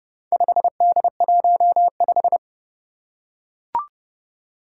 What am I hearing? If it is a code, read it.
Morse code, heard as a keyed tone, about 30 wpm, 700 Hz. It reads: 5D15
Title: Four Challenging Letter-Number Combinations 30wpm